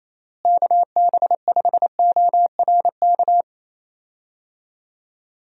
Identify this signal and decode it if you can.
Morse code, heard as a keyed tone, about 27 wpm, 700 Hz. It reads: KB5ORK
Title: USA Call Signs - 6 Characters - Repeated 2x - 28wpm